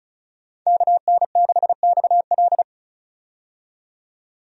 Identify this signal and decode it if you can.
Morse code, heard as a keyed tone, about 36 wpm, 700 Hz. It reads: KN6XL
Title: USA Call Signs - 5 Characters - 35wpm